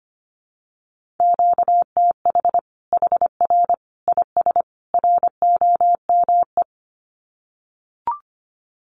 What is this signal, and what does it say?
Morse code, heard as a keyed tone, about 25 wpm, 700 Hz. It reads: QTH HR IS ROME
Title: Common QSO Phrases 25wpm